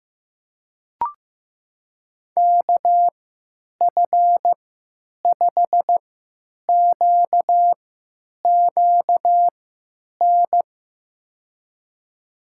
Morse code, heard as a keyed tone, about 15 wpm, spaced as for 9 wpm, 700 Hz. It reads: KF5QQN
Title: USA Call Signs - 6 Characters - 3x Character Spacing - 15wpm